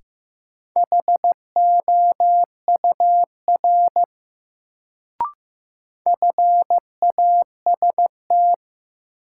Morse code, heard as a keyed tone, about 15 wpm, 700 Hz. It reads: HOUR FAST
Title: Sentences from Top 400 Words - New Word Drill - 15wpm